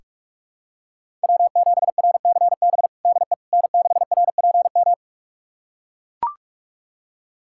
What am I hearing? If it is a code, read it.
Morse code, heard as a keyed tone, about 43 wpm, 700 Hz. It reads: W7RCB DE N6RPM
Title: USA Callsign DE Callsign - 5 Characters - 45wpm